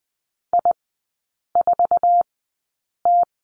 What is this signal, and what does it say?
Morse code, heard as a keyed tone, about 20 wpm, spaced as for 8 wpm, 700 Hz. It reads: I4T
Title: 107 Lesson 18e - Letter m - Three Characters 20wpm